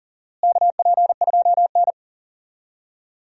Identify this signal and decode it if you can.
Morse code, heard as a keyed tone, about 40 wpm, 700 Hz. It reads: KP2N
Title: DXCC-100 Call Signs - 4 Characters - Repeated 3x - 40wpm